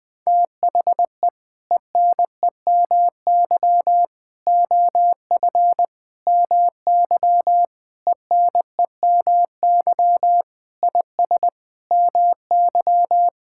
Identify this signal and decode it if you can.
Morse code, heard as a keyed tone, about 19 wpm, 700 Hz. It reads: THE ENEMY OF MY ENEMY IS MY
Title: Sentences with 8-Word Phrases - 20wpm